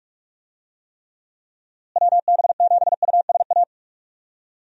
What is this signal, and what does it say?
Morse code, heard as a keyed tone, about 46 wpm, 700 Hz. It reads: WB7USA
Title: USA Call Signs - 6 Characters - Repeated 3x - 45wpm